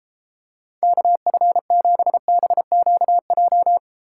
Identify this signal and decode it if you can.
Morse code, heard as a keyed tone, about 34 wpm, 700 Hz. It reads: KF7BQJ